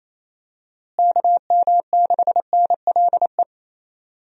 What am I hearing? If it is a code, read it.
Morse code, heard as a keyed tone, about 28 wpm, 700 Hz. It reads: KM6NLE
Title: USA Call Signs - 6 Characters - Repeated 3x - 28wpm